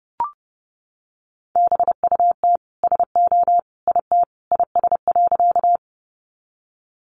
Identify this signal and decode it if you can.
Morse code, heard as a keyed tone, about 30 wpm, 700 Hz. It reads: BUT SO IT IS.
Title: Sentences from Top 100 Words 30wpm